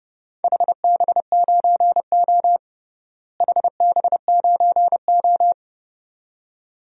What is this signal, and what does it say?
Morse code, heard as a keyed tone, about 30 wpm, 700 Hz. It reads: HB9O HB9O
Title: DXCC-100 Call Signs - 4 Characters - Repeated 2x - 30wpm